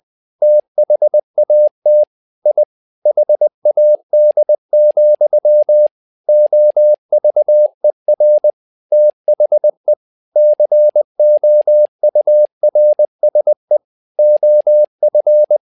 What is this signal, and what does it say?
Morse code, heard as a keyed tone, about 20 wpm, 600 Hz. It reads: THAT I HAD, OVER THE COURSE OF
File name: learning center / tower trouble / tower trouble 20wpm